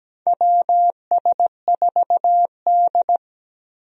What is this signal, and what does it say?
Morse code, heard as a keyed tone, about 17 wpm, 700 Hz. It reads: WS4D
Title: USA Call Signs - 4 Characters - Repeated 2x - 17wpm